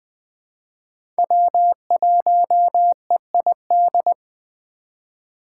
Morse code, heard as a keyed tone, about 20 wpm, 700 Hz. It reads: W1EID
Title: USA Call Signs - 6x Character Spacing - 20wpm